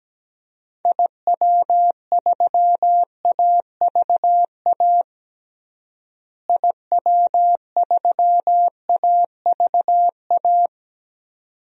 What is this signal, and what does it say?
Morse code, heard as a keyed tone, about 17 wpm, 700 Hz. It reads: IW3AVA IW3AVA
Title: DXCC-100 Call Signs - 6 Characters - Repeated 2x - 17wpm